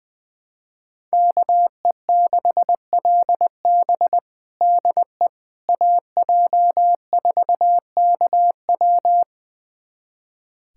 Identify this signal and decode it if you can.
Morse code, heard as a keyed tone, about 20 wpm, 700 Hz. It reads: KE6LB DE AJ4KW
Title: USA Callsign DE Callsign - 5 Characters - 20wpm